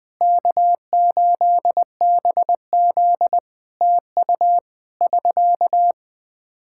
Morse code, heard as a keyed tone, about 20 wpm, 700 Hz. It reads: K8BZ TU <SK>
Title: Practice QSO - Exchange 4 - Ending - 20wpm